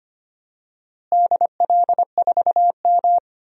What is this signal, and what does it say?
Morse code, heard as a keyed tone, about 25 wpm, 700 Hz. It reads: DL4M
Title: DXCC-100 Call Signs - 4 Characters - Repeated 2x - 25wpm